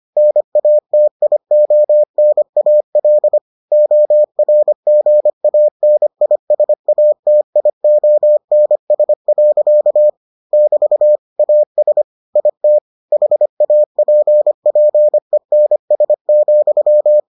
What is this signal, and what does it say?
Morse code, heard as a keyed tone, about 25 wpm, 600 Hz. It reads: NATIONAL ORGANISATIONS. <BT> AS IT HAPPENS,